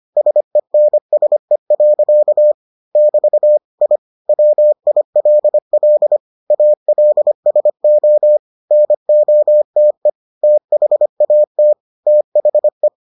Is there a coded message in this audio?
Yes: Morse code from 0.2 to 12.9 s, heard as a keyed tone, about 25 wpm, 600 Hz. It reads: SENSE. <BT> I WILL ALSO NOTE THAT THE